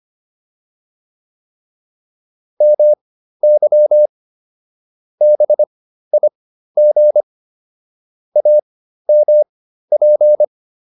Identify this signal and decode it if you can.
Morse code, heard as a keyed tone, about 25 wpm, spaced as for 13 wpm, 600 Hz. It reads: MY BIG AMP